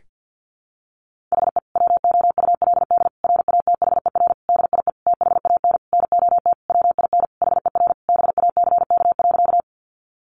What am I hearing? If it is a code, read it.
Morse code, heard as a keyed tone, about 50 wpm, 700 Hz. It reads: HE WOULD RATHER DIE THAN NOT WIN HER BACK.
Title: Sentences from Top 1500 Words 50wpm